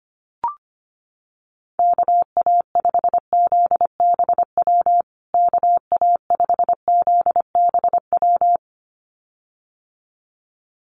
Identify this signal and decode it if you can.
Morse code, heard as a keyed tone, about 25 wpm, 700 Hz. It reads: KA5ZBW KA5ZBW
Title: USA Call Signs Repeated 2x 25wpm